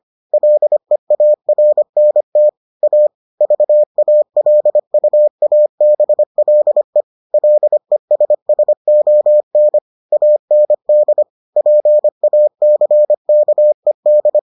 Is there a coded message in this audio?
Yes: Morse code from 0.3 to 14.4 s, heard as a keyed tone, about 25 wpm, 600 Hz. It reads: LEARNT A VALUABLE LESSON AND PACKED